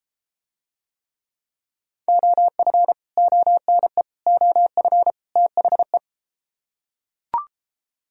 Morse code, heard as a keyed tone, about 32 wpm, 700 Hz. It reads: OF ONE OF THE